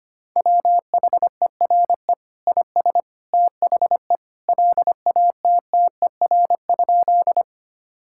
Morse code, heard as a keyed tone, about 25 wpm, 700 Hz. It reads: WHERE IS THE LATTER?
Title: Sentences from Top 1600 Words 25wpm